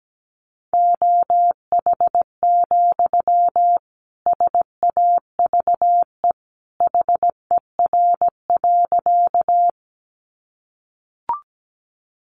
Morse code, heard as a keyed tone, about 17 wpm, 700 Hz. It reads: OH, SAVE HER.
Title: Sentences from Top 1100 Words 17wpm